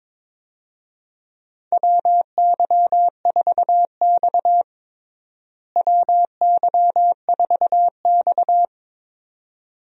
Morse code, heard as a keyed tone, about 22 wpm, 700 Hz. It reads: WY4X WY4X
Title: USA Call Signs - 4 Characters - Repeated 2x - 22wpm